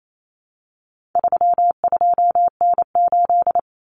Morse code, heard as a keyed tone, about 28 wpm, 700 Hz. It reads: 32N8
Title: Common Cut Numbers - 4-Digit 28wpm